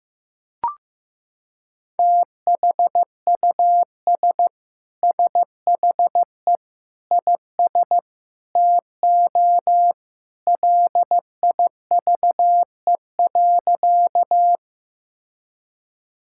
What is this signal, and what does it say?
Morse code, heard as a keyed tone, about 15 wpm, 700 Hz. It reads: THUS SHE IS TO LIVE.
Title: Sentences from Top 1000 Words - Review Part 5 - 15wpm